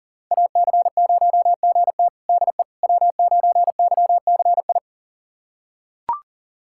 Morse code, heard as a keyed tone, about 40 wpm, 700 Hz. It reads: AC0GT DE W9YCI
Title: USA Callsign DE Callsign - 5 Characters - 40wpm